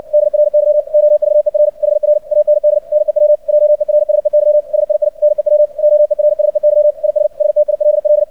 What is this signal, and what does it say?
Morse code, heard as a keyed tone, about 56 wpm, 600 Hz. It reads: DRY OVER HR AND NEW JERSEY SII NEW OERSEY IS STILL
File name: archived / CRSnet